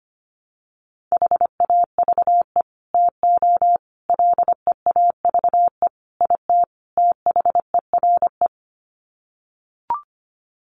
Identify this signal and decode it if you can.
Morse code, heard as a keyed tone, about 25 wpm, 700 Hz. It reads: HAVE TO LEAVE IT THERE
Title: Sets of 5 Words Encore2 25wpm